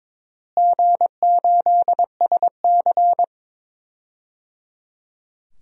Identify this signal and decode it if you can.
Morse code, heard as a keyed tone, about 22 wpm, 700 Hz. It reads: G8SC
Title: DXCC-100 Call Signs - 4 Characters - Repeated 3x - 22wpm